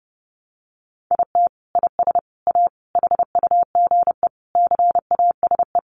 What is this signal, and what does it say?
Morse code, heard as a keyed tone, about 30 wpm, 700 Hz. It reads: IT IS A HUGE CASE
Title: Sentences from Top 1100 Words 30wpm